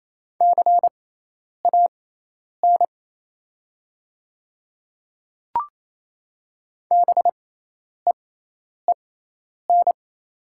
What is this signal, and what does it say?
Morse code, heard as a keyed tone, about 28 wpm, spaced as for 10 wpm, 700 Hz. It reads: CAN BEEN
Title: 6x Character Spacing - Top 100 Words - 28wpm